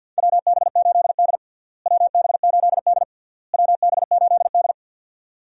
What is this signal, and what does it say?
Morse code, heard as a keyed tone, about 46 wpm, 700 Hz. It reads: WB8D WB8D WB8D
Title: USA Call Signs - 4 Characters - Repeated 3x - 50wpm